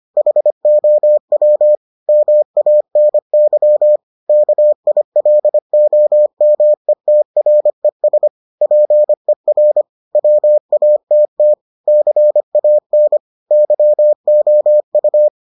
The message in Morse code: HOW MANY KILOMETRES PER WATT CAN YOU

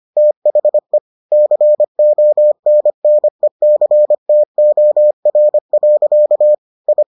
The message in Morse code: THE CONNECTOR. I

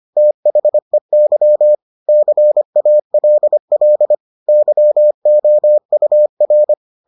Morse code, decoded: THEY CALL YOUR

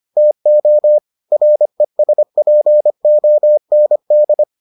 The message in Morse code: TO RESPOND

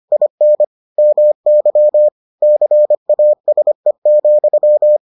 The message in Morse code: IN MY CASE,

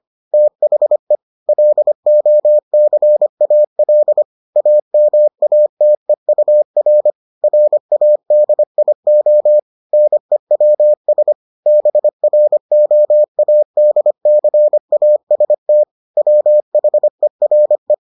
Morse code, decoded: THE LOCAL AMATEUR RADIO NEWS BROADCAST WHERE

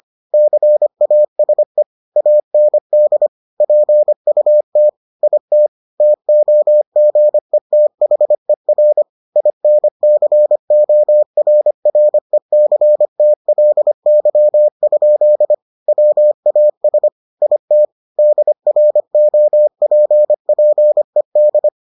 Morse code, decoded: CASE AND PUT IT TOGETHER INCORRECTLY? WAS IT DROPPED

